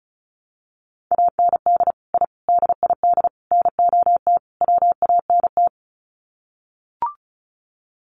Morse code, heard as a keyed tone, about 34 wpm, 700 Hz. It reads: AND I DID NOT WANT
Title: Sets of 5 Words 35wpm